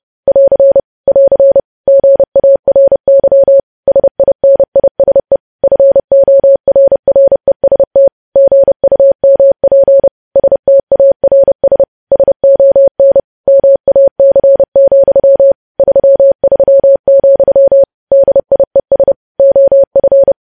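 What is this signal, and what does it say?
Morse code, heard as a keyed tone, about 30 wpm, 550 Hz. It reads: <AR> <AR> GARY SINISE FORREST GUMP STARS SON MAC, 33, DIES OF
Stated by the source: News headlines sent in Morse code at 30.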